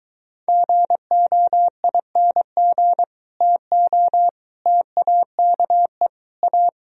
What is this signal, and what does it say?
Morse code, heard as a keyed tone, about 24 wpm, 700 Hz. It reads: GOING TO TAKE A